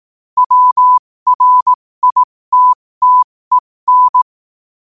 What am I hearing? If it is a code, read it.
Morse code, heard as a keyed tone, about 18 wpm, spaced as for 15 wpm, 1000 Hz. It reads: WRITTEN
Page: New quotes every day in morse code at 15 Words per minute.